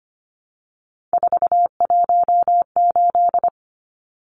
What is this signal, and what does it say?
Morse code, heard as a keyed tone, about 25 wpm, 700 Hz. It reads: 418
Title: Numbers - 3-Digit 25wpm